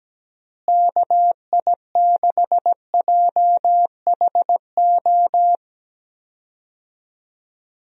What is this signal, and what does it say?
Morse code, heard as a keyed tone, about 17 wpm, 700 Hz. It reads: KI6JHO
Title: USA Call Signs - 6 Characters - Repeated 2x - 17wpm